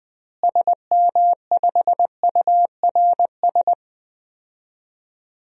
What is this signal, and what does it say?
Morse code, heard as a keyed tone, about 20 wpm, 700 Hz. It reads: SM5URS
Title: DXCC-100 Call Signs - 6 Characters - Repeated 3x - 20wpm